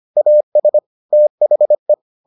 Morse code, heard as a keyed tone, about 25 wpm, 600 Hz. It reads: AS THE